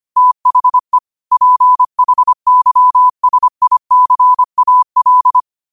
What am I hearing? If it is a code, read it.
Morse code, heard as a keyed tone, about 25 wpm, 1000 Hz. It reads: THE PHYSICAL
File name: qotdfiles / 25wpm